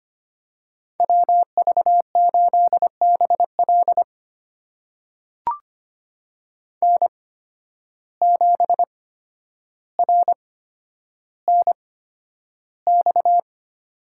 Morse code, heard as a keyed tone, about 25 wpm, spaced as for 11 wpm, 700 Hz. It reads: WV8BL N7RNX
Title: USA Call Signs - 5 Characters - 8x Character Spacing - 25wpm